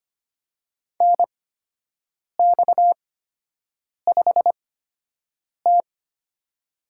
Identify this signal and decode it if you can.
Morse code, heard as a keyed tone, about 25 wpm, spaced as for 7 wpm, 700 Hz. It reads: NX5T
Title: USA Call Signs - 8x Character Spacing - 25wpm